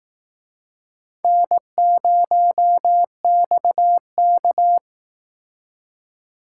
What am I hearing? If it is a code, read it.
Morse code, heard as a keyed tone, about 18 wpm, 700 Hz. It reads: N0XK
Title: Speed Racing - USA 1x2 Call Signs 15wpm